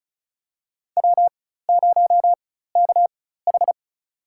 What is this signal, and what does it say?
Morse code, heard as a keyed tone, about 34 wpm, spaced as for 16 wpm, 700 Hz. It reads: W0KH
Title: USA Call Signs - 4 Characters - 4x Character Spacing - 35wpm